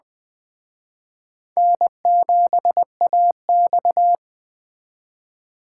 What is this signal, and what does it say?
Morse code, heard as a keyed tone, about 21 wpm, 700 Hz. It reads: N7AX